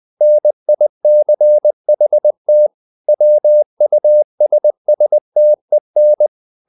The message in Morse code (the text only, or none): NICHT WUSSTEN